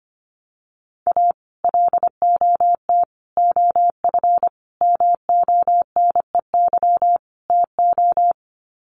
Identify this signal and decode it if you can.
Morse code, heard as a keyed tone, about 25 wpm, 700 Hz. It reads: A LOT OF MONEY TO
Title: Sets of 5 Words Encore2 25wpm